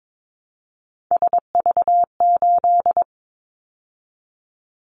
Morse code, heard as a keyed tone, about 22 wpm, 700 Hz. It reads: SV8
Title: Callsign Prefixes for Top 50 DXCC - 22wpm